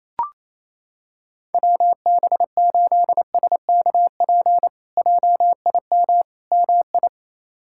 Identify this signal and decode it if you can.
Morse code, heard as a keyed tone, about 28 wpm, 700 Hz. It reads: WB8SKP JIM MI
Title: K1USN SST QSO Practice - 28wpm